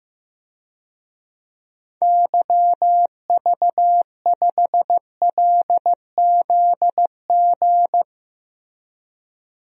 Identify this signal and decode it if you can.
Morse code, heard as a keyed tone, about 15 wpm, 700 Hz. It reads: YV5LZG
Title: DXCC-100 Call Signs - 6 Characters - Repeated 3x - 15wpm